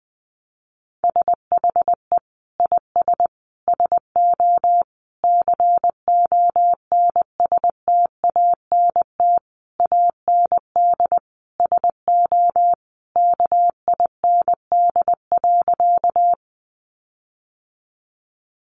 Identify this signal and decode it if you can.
Morse code, heard as a keyed tone, about 20 wpm, 700 Hz. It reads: SHE IS SO CONSTANT AND SO KIND.
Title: Sentences from Top 1500 Words 20wpm